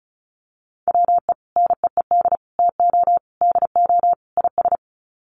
Morse code, heard as a keyed tone, about 34 wpm, 700 Hz. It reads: WE NEED TO DO IS
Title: Sets of 5 Words Encore2 35wpm